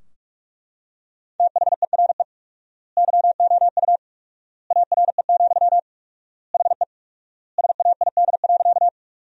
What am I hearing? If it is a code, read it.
Morse code, heard as a keyed tone, about 42 wpm, 700 Hz. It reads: THERE YOU ARE, HE SAID.
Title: 4x Word-Spacing - Sentences from Top 100 Words - 45wpm